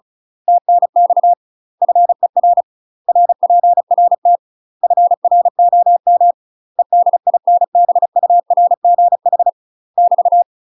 Morse code, heard as a keyed tone, about 34 wpm, 700 Hz. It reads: TNX FER RPRT FROM EDINBURGH <BT>
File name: UK MASTER QSOs TRAINING / 35WPM / UK MASTER QSOs 0024